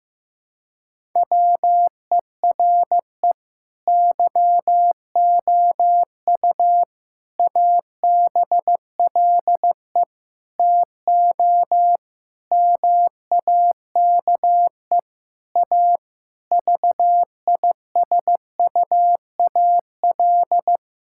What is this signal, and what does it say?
Morse code, heard as a keyed tone, about 16 wpm, 700 Hz. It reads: WERE YOU ABLE TO MAKE A VISUAL
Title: Sentences from Top 1000 Words - Review Part 6 - 15wpm